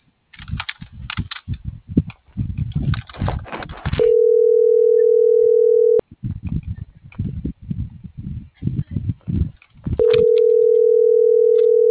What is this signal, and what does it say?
Ringback tone (North America), 440 + 480 Hz.